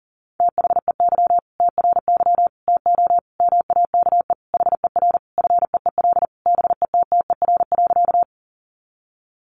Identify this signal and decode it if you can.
Morse code, heard as a keyed tone, about 40 wpm, 700 Hz. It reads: THEY TRY TO MAKE HER FEEL BETTER.
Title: Easy Phrases 40wpm